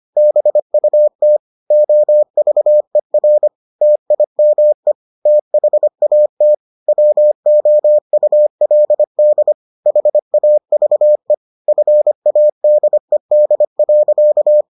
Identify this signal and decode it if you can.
Morse code, heard as a keyed tone, about 25 wpm, 600 Hz. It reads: BUT OVER TIME THAT WOULD HAVE FADED.